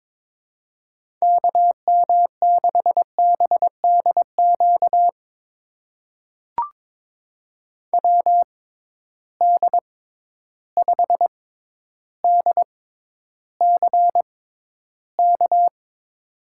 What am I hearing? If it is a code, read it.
Morse code, heard as a keyed tone, about 23 wpm, spaced as for 11 wpm, 700 Hz. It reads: KM6BDQ WD5DCK
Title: USA Call Signs - 6 Characters - 6x Character Spacing - 22wpm